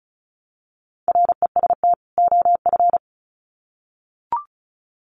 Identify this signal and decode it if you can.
Morse code, heard as a keyed tone, about 34 wpm, 700 Hz. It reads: REST OF
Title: Sets of 2 Words 35wpm